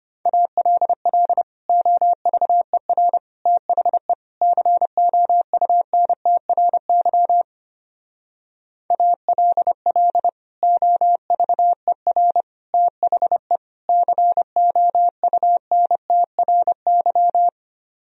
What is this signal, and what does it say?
Morse code, heard as a keyed tone, about 31 wpm, 700 Hz. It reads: ALL OVER THE COUNTRY ALL OVER THE COUNTRY
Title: Speed-Racing - Sets of 4 Words 25wpm